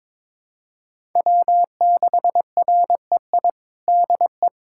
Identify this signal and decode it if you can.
Morse code, heard as a keyed tone, about 21 wpm, 700 Hz. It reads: W6REI DE
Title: USA Callsign DE Callsign 22wpm